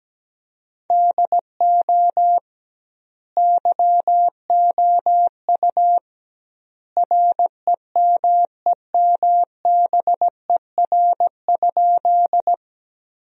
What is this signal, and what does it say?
Morse code, heard as a keyed tone, about 17 wpm, 700 Hz. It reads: DO YOU REMEMBER?
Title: 2x Word-Spacing - Sentences from Top 400 Words - 17wpm